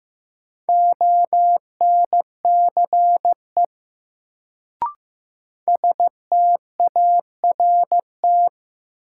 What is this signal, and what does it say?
Morse code, heard as a keyed tone, about 15 wpm, 700 Hz. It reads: ONCE START
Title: Sentences from Top 300 Words - New Word Drill - 15wpm